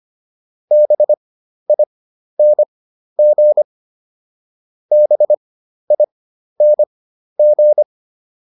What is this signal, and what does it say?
Morse code, heard as a keyed tone, about 25 wpm, spaced as for 12 wpm, 600 Hz. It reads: BING BING